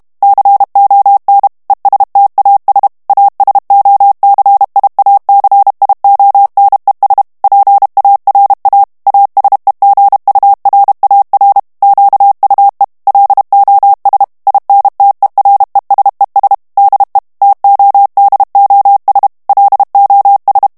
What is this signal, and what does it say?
Morse code, heard as a keyed tone, about 32 wpm, 800 Hz. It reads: CON ESTAS ASOCIACIONES PARA ASEGURAR QUE LOS INTERESES DE TODOS LOS
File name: morse mp3 / ESPAGNOL TEXTE EN CLAIR 1 VITESSE 30 MOTS MINUTES